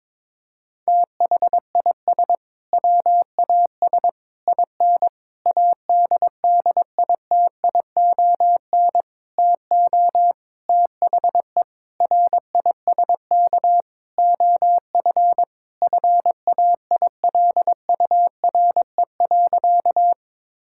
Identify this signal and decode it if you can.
Morse code, heard as a keyed tone, about 22 wpm, 700 Hz. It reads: THIS WAS IN ADDITION TO THE RISK OF FAILURE.
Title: Sentences from Top 1000 Words - Review Part 6 - 22wpm